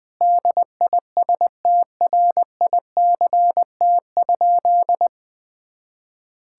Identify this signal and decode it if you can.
Morse code, heard as a keyed tone, about 20 wpm, 700 Hz. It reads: DISTRICT?
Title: Sentences from Top 1600 Words 20wpm